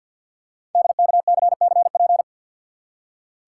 Morse code, heard as a keyed tone, about 48 wpm, 700 Hz. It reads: DKCCP
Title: Code Groups - 5 Letters - 50wpm